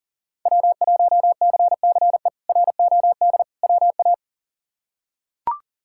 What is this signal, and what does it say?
Morse code, heard as a keyed tone, about 40 wpm, 700 Hz. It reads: W1CCE ROD WA
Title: K1USN SST QSO Practice - 40wpm